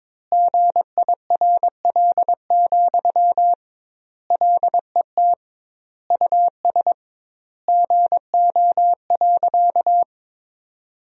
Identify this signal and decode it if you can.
Morse code, heard as a keyed tone, about 22 wpm, 700 Hz. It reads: GIRL, LET US GO.
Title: 2x Word-Spacing - Sentences from Top 400 Words - 22wpm